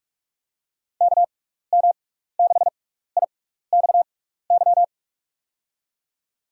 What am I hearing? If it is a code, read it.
Morse code, heard as a keyed tone, about 46 wpm, spaced as for 16 wpm, 700 Hz. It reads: KM6IXY
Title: USA Call Signs - 6 Characters - 6x Character Spacing - 45wpm